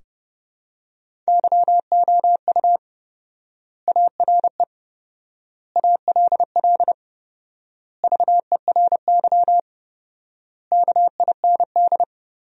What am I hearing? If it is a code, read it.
Morse code, heard as a keyed tone, about 32 wpm, 700 Hz. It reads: YOU ARE ALL VERY KIND
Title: 4x Word-Spacing - Sentences from Top 200 Words - 30wpm